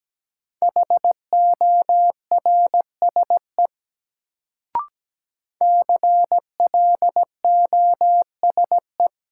Morse code, heard as a keyed tone, about 17 wpm, 700 Hz. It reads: HORSE CLOSE
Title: Sentences from Top 300 Words - New Word Drill - 17wpm